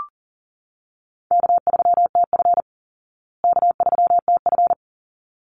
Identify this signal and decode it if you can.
Morse code, heard as a keyed tone, about 40 wpm, 700 Hz. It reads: K3TF K3TF
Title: Easy Call Signs Repeated 2x - 40wpm